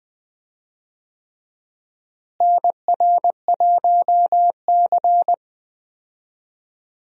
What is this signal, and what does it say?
Morse code, heard as a keyed tone, about 20 wpm, 700 Hz. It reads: NR1C
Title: USA Call Signs - 4 Characters - Repeated 3x - 20wpm